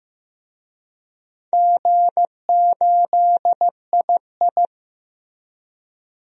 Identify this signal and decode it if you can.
Morse code, heard as a keyed tone, about 15 wpm, 700 Hz. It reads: G8II
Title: DXCC-100 Call Signs - 4 Characters - Repeated 3x - 15wpm